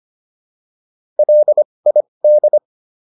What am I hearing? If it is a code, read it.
Morse code, heard as a keyed tone, about 25 wpm, spaced as for 20 wpm, 600 Hz. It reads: LID